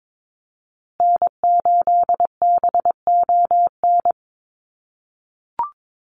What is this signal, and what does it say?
Morse code, heard as a keyed tone, about 22 wpm, 700 Hz. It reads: N8BON
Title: USA Call Signs Repeated 2x 22wpm